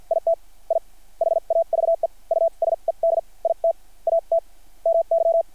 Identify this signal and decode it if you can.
Morse code, heard as a keyed tone, about 50 wpm, spaced as for 41 wpm, 660 Hz. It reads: IT I HAVE USED IT AT MY
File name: archived / CRSnet